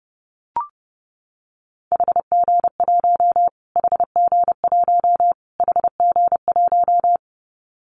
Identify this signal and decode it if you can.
Morse code, heard as a keyed tone, about 30 wpm, 700 Hz. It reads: HG1 HG1 HG1
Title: Callsign Prefixes for Top 25 DXCC - Repeated 3x - 30wpm